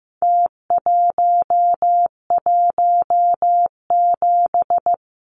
Morse code, heard as a keyed tone, about 15 wpm, 700 Hz. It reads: T117
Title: Numbers - Contest Style - 4-Digit 15wpm